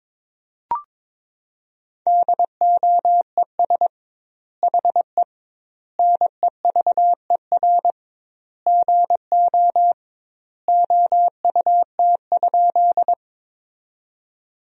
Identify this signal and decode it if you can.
Morse code, heard as a keyed tone, about 22 wpm, 700 Hz. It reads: DOES HE NEVER GO OUT?
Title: 2x Word-Spacing - Sentences from Top 300 Words - 22wpm